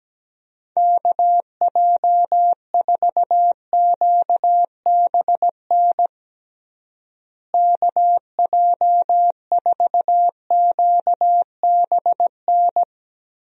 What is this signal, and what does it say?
Morse code, heard as a keyed tone, about 17 wpm, 700 Hz. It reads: KJ4QBN KJ4QBN
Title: USA Call Signs - 6 Characters - Repeated 2x - 17wpm